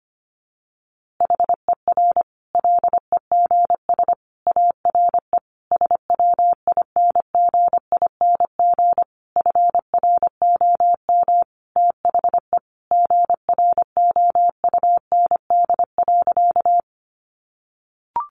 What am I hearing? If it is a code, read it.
Morse code, heard as a keyed tone, about 25 wpm, 700 Hz. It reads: HER LEGS ARE SWINGING FROM THE GROUND.
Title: Easy Phrases 25wpm